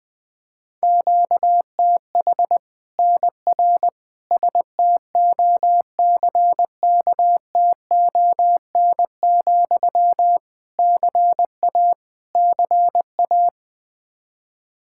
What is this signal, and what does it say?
Morse code, heard as a keyed tone, about 20 wpm, 700 Hz. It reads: QTH NR STOCKTON, CA CA
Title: Practice QSO - Exchange 1 - Beginning - 20wpm